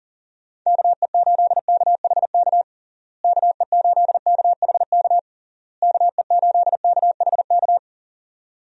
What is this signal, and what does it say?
Morse code, heard as a keyed tone, about 40 wpm, 700 Hz. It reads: KE8KHK KE8KHK KE8KHK
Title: USA Call Signs - 6 Characters - Repeated 3x - 40wpm